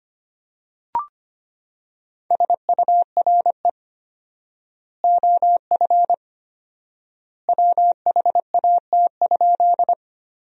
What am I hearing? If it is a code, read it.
Morse code, heard as a keyed tone, about 25 wpm, 700 Hz. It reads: SURE OF WHAT?
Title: 4x Word-Spacing - Sentences from Top 300 Words - 25wpm